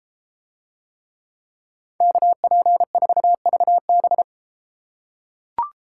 Morse code, heard as a keyed tone, about 33 wpm, 700 Hz. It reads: KP4VB